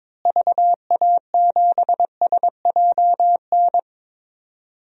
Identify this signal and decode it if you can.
Morse code, heard as a keyed tone, about 22 wpm, 700 Hz. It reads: VA7SJN
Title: DXCC-100 Call Signs - 6 Characters - Repeated 3x - 22wpm